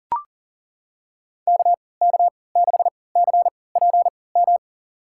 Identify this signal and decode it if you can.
Morse code, heard as a keyed tone, about 39 wpm, spaced as for 22 wpm, 700 Hz. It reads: KK6CPM
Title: USA Call Signs - 6 Characters - 3x Character Spacing - 40wpm